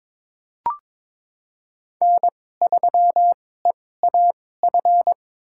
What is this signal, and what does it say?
Morse code, heard as a keyed tone, about 22 wpm, spaced as for 16 wpm, 690 Hz. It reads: N3EAF
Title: USA Call Signs - 5 Characters - 2x Character Spacing - 22wpm